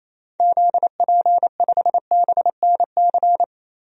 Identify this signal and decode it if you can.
Morse code, heard as a keyed tone, about 28 wpm, 700 Hz. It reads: ZP5BNC
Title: DXCC-100 Call Signs - 6 Characters - 28wpm